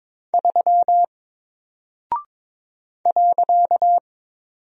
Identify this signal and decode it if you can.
Morse code, heard as a keyed tone, about 22 wpm, 700 Hz. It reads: 3 .
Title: Single Character (Letter:Number:Punctuation:Prosign) Alt 22wpm